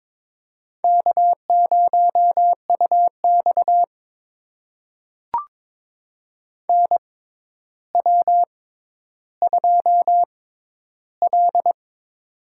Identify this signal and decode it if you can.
Morse code, heard as a keyed tone, about 22 wpm, spaced as for 11 wpm, 700 Hz. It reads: K0UX NW2L
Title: USA Call Signs - 4 Characters - 6x Character Spacing - 22wpm